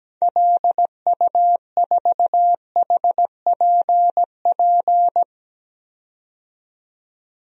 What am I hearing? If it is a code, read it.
Morse code, heard as a keyed tone, about 17 wpm, 700 Hz. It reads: LU4HPP
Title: DXCC-100 Call Signs - 6 Characters - 17wpm